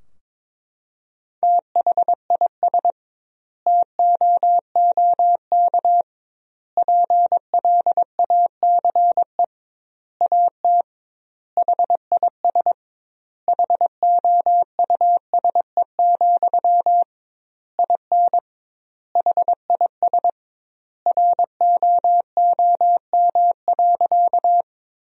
Morse code, heard as a keyed tone, about 22 wpm, 700 Hz. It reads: THIS TOOK PLACE AT HIS HOUSE, IN HIS ROOM.
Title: 2x Word-Spacing - Sentences from Top 300 Words - 22wpm